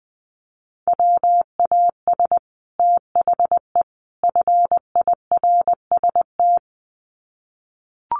Morse code, heard as a keyed tone, about 20 wpm, 700 Hz. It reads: WAS THE FIRST
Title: Sets of 3 Words 20wpm